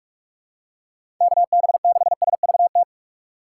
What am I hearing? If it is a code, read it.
Morse code, heard as a keyed tone, about 46 wpm, 700 Hz. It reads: KB6SVT
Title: USA Call Signs - 6 Characters - 3x Character Spacing - 45wpm